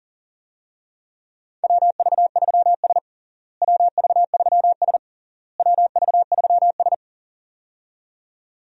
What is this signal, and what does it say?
Morse code, heard as a keyed tone, about 40 wpm, 700 Hz. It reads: WV3S WV3S WV3S
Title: USA Call Signs - 4 Characters - Repeated 3x - 40wpm